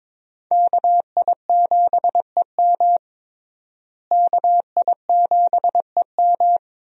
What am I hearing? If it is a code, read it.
Morse code, heard as a keyed tone, about 21 wpm, 700 Hz. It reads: KI7EM KI7EM
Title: USA Call Signs - 5 Characters - Repeated 2x - 22wpm